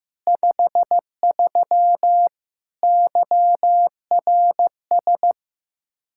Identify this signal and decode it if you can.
Morse code, heard as a keyed tone, about 15 wpm, 700 Hz. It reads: 53 YRS
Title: Practice QSO - Exchange 3 - More Info - 15wpm